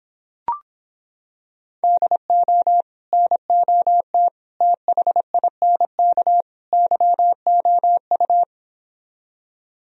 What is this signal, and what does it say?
Morse code, heard as a keyed tone, about 26 wpm, 700 Hz. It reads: DO NOT THINK YOU